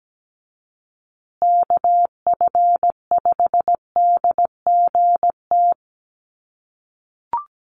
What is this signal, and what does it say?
Morse code, heard as a keyed tone, about 17 wpm, 700 Hz. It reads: KF5DGT
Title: USA Call Signs Repeated 3x 17wpm